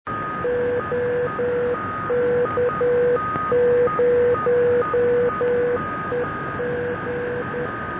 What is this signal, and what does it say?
Morse code, heard as a keyed tone, about 10 wpm, 490 Hz, with noise. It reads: OK0EG